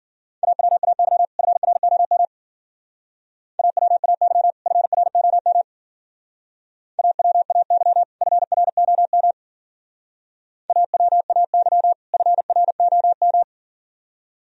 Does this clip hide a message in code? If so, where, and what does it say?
Yes: Morse code from 0.4 to 13.4 s, heard as a keyed tone, about 50 wpm, 700 Hz. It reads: AWAY FROM AWAY FROM AWAY FROM AWAY FROM